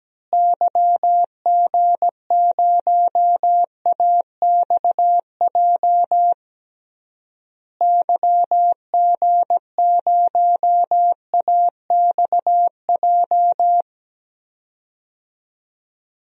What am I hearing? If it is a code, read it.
Morse code, heard as a keyed tone, about 17 wpm, 700 Hz. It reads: YG0AXJ YG0AXJ
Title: DXCC-100 Call Signs - 6 Characters - Repeated 3x - 17wpm